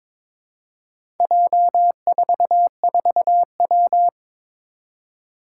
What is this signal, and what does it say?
Morse code, heard as a keyed tone, about 22 wpm, 700 Hz. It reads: J44W
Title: Four Challenging Letter-Number Combinations 22wpm